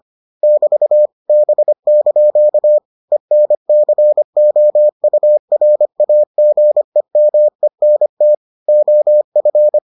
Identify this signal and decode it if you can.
Morse code, heard as a keyed tone, about 25 wpm, 600 Hz. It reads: <BT> B) ENCOURAGEMENT OF